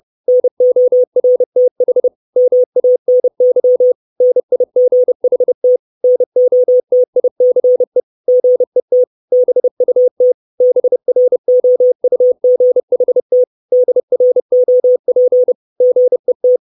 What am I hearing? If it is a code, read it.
Morse code, heard as a keyed tone, about 30 wpm, 500 Hz. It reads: NORTH MANY NIGHT NOTICE GET BUT BROUGHT DROP GET
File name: 632 Words 30WPM